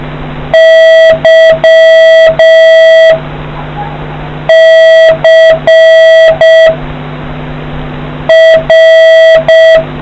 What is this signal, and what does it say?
Morse code, heard as a keyed tone, about 6 wpm, 650 Hz, with noise. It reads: Y C R